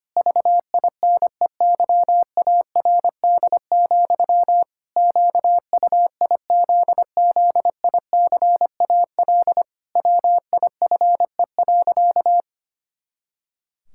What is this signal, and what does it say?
Morse code, heard as a keyed tone, about 25 wpm, 700 Hz. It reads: VINEYARD, QUIZZICAL WIFE.
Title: Pangram Challenge v2 - 25wpm